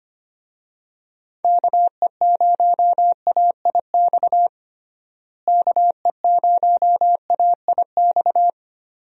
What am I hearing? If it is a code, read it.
Morse code, heard as a keyed tone, about 25 wpm, 700 Hz. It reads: KE0AIX KE0AIX
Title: USA Call Signs - 6 Characters - Repeated 2x - 25wpm